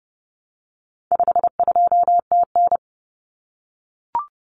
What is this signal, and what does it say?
Morse code, heard as a keyed tone, about 30 wpm, 700 Hz. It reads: 52TN
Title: Common Cut Numbers - 4-Digit 30wpm